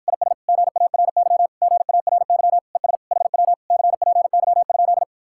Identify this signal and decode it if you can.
Morse code, heard as a keyed tone, about 53 wpm, 700 Hz. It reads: IS GARY GARY ES HW CPY?